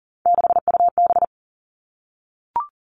Morse code, heard as a keyed tone, about 39 wpm, 700 Hz. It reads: 6UB
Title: Three Challenging Letter-Number Combinations 40wpm